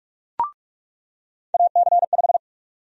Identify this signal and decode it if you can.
Morse code, heard as a keyed tone, about 46 wpm, 700 Hz. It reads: AC5
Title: USA Call Sign Prefixes - 45wpm